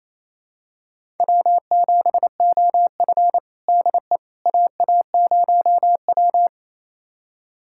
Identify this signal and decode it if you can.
Morse code, heard as a keyed tone, about 28 wpm, 700 Hz. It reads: W7OF DE AA0W
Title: USA Callsign DE Callsign - 4 Characters - 28wpm